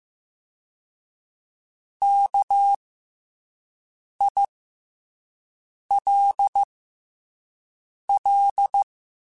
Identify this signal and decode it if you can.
Morse code, heard as a keyed tone, about 15 wpm, 800 Hz. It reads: K I L L
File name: Special INGs-mp3s / Special INGs-mp3s / SlowToFaster ing @10